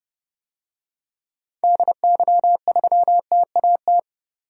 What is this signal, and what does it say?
Morse code, heard as a keyed tone, about 31 wpm, 700 Hz. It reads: DY3TAT
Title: DXCC-100 Call Signs - 6 Characters - 30wpm